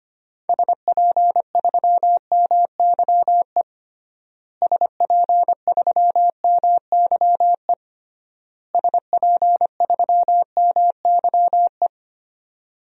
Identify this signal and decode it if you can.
Morse code, heard as a keyed tone, about 25 wpm, 700 Hz. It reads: SP3MYE SP3MYE SP3MYE
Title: DXCC-100 Call Signs - 6 Characters - Repeated 3x - 25wpm